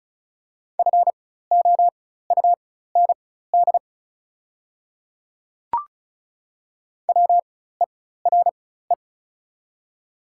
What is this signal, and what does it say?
Morse code, heard as a keyed tone, about 36 wpm, spaced as for 16 wpm, 690 Hz. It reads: FOUND WERE
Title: 4x Character Spacing - Top 300 Words - 35wpm